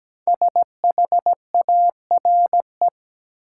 Morse code, heard as a keyed tone, about 17 wpm, 700 Hz. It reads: SHARE
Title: Sentences from Top 900 Words - New Word Drill - 17wpm